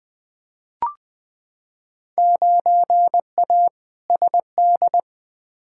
Morse code, heard as a keyed tone, about 20 wpm, 700 Hz. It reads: 9A SD
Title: Field Day Exchanges Encore - 20wpm